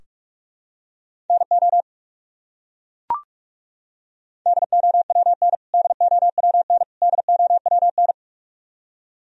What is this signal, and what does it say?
Morse code, heard as a keyed tone, about 41 wpm, 700 Hz. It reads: NO DOWN DOWN DOWN
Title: Top 100 Words Repeated 45wpm